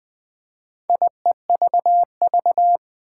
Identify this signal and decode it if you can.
Morse code, heard as a keyed tone, about 20 wpm, 700 Hz. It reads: IEVV
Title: Four Challenging Letter-Number Combinations 20wpm